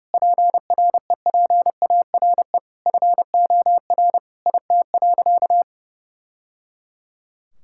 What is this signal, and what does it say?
Morse code, heard as a keyed tone, about 30 wpm, 700 Hz. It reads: PREPARE FOR IT.
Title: Sentences from Top 1000 Words - Review Part 6 - 30wpm